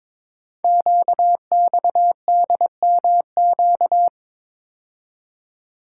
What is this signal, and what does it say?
Morse code, heard as a keyed tone, about 22 wpm, 700 Hz. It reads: QXDMQ
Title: Code Groups - 5 Letters - 22wpm